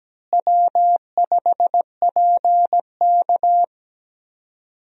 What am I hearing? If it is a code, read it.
Morse code, heard as a keyed tone, about 17 wpm, 700 Hz. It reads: W5PK
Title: USA Call Signs - 4 Characters - Repeated 3x - 17wpm